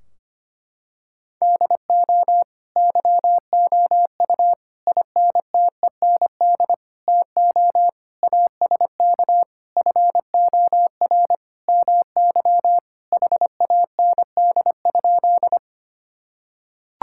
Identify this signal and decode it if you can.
Morse code, heard as a keyed tone, about 25 wpm, 700 Hz. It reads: DO YOU INTEND TO ASK FOR MY HAND?
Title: Sentences from Top 1600 Words 25wpm